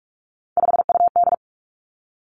Morse code, heard as a keyed tone, about 46 wpm, 700 Hz. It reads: 5UD
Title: Three Challenging Letter-Number Combinations 45wpm